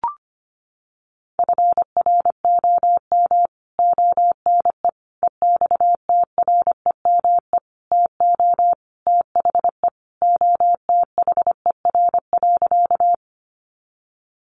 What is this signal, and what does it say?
Morse code, heard as a keyed tone, about 25 wpm, 700 Hz. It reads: FROM ONE EXTREME TO THE OTHER.
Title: Sentences from Top 1300 Words 25wpm